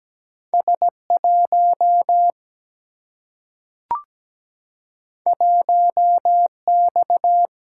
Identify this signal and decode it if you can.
Morse code, heard as a keyed tone, about 17 wpm, 700 Hz. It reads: S1 1X
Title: Single-Letter Single-Number 17wpm